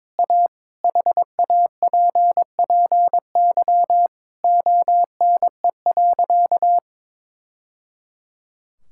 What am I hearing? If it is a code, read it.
Morse code, heard as a keyed tone, about 22 wpm, 700 Hz. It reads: A HAPPY ONE.
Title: Sentences from Top 1000 Words - Review Part 5 - 22wpm